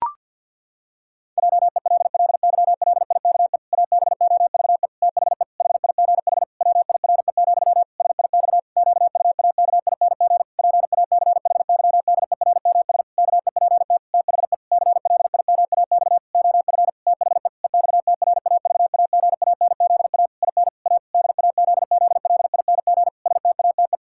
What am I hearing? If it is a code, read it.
Morse code, heard as a keyed tone, about 50 wpm, 700 Hz. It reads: JELLYLIKE ABOVE THE HIGH WIRE, SIX QUAKING PACHYDERMS KEPT THE CLIMAX OF THE EXTRAVAGANZA IN A DAZZLING STATE
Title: Pangram Challenge v2 - 50wpm